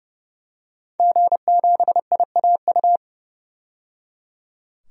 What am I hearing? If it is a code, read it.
Morse code, heard as a keyed tone, about 30 wpm, 700 Hz. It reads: G7IAU
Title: DXCC-100 Call Signs - 5 Characters - 30wpm